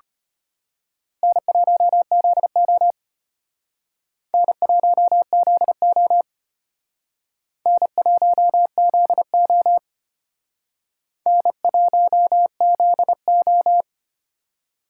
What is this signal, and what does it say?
Morse code, heard as a keyed tone, about 32 wpm, 700 Hz. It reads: N1ZO N1ZO N1ZO N1ZO